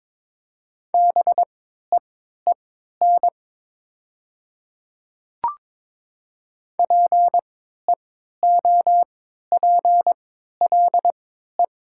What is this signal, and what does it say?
Morse code, heard as a keyed tone, about 22 wpm, spaced as for 12 wpm, 700 Hz. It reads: BEEN PEOPLE
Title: 3x Character Spacing - Top 100 Words - 22wpm